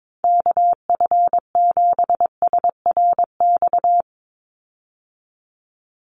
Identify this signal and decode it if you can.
Morse code, heard as a keyed tone, about 22 wpm, 700 Hz. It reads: KF7SRX
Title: USA Call Signs 22wpm